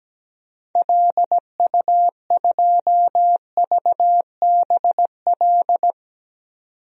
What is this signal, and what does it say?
Morse code, heard as a keyed tone, about 17 wpm, 700 Hz. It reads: LU2VBL
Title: DXCC-100 Call Signs - 6 Characters - 17wpm